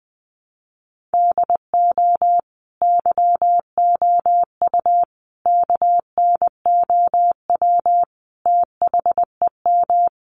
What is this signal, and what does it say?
Morse code, heard as a keyed tone, about 20 wpm, 700 Hz. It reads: DO YOU KNOW THEM
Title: Sentences from Top 1400 Words 20wpm